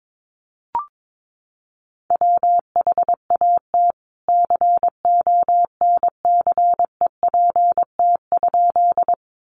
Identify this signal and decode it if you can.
Morse code, heard as a keyed tone, about 22 wpm, 700 Hz. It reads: WHAT CONCEPT?
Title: Sentences from Top 1100 Words 22wpm